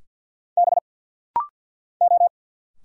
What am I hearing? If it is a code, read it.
Morse code, heard as a keyed tone, about 50 wpm, 700 Hz. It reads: B O
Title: Single Letters - Rapid-Fire 50wpm